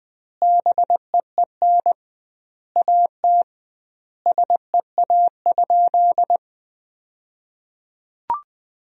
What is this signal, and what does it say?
Morse code, heard as a keyed tone, about 20 wpm, 700 Hz. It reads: BEEN AT SEA?
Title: 2x Word-Spacing - Sentences from Top 300 Words - 20wpm